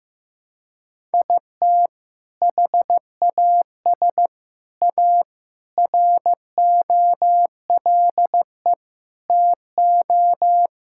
Morse code, heard as a keyed tone, about 16 wpm, 700 Hz. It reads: IT HAS A ROLE TO
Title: Sentences from Top 1000 Words - Review Part 6 - 15wpm